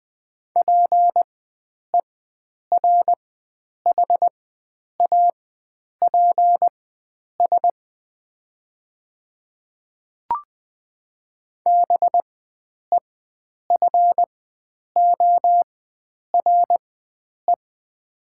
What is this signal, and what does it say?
Morse code, heard as a keyed tone, about 20 wpm, spaced as for 9 wpm, 700 Hz. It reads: PERHAPS BEFORE
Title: 4x Character Spacing - Top 500 Words - 20wpm